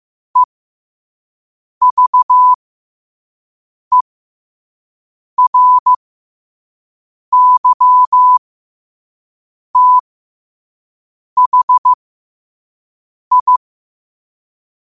Morse code, heard as a keyed tone, about 15 wpm, spaced as for 5 wpm, 1000 Hz. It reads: EVERYTHI
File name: qotdfiles / 5wpm